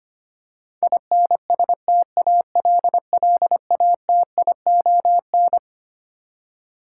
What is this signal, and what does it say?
Morse code, heard as a keyed tone, about 25 wpm, 700 Hz. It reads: INSTALLATION
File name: Twelve Letter Words - 25wpm